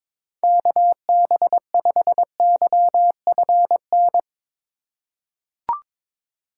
Morse code, heard as a keyed tone, about 22 wpm, 700 Hz. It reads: KB5YFN
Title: USA Call Signs - 4x Character Spacing - 22wpm